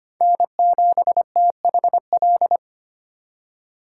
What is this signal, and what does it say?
Morse code, heard as a keyed tone, about 32 wpm, 700 Hz. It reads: N7THL